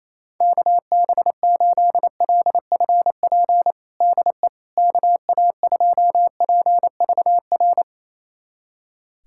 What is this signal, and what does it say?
Morse code, heard as a keyed tone, about 28 wpm, 700 Hz. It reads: KB8LFP DE KA2PVR
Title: USA Callsign DE Callsign - 6 Characters - 28wpm